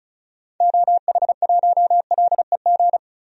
Morse code, heard as a keyed tone, about 34 wpm, 700 Hz. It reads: OH1LEG
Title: DXCC-100 Call Signs - 6 Characters - 35wpm